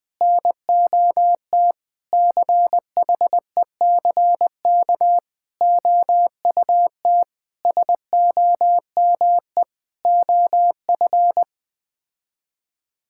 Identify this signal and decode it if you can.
Morse code, heard as a keyed tone, about 21 wpm, 700 Hz. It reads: NOT CHECK OUT SOME OF
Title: Sets of 9 Words - 20wpm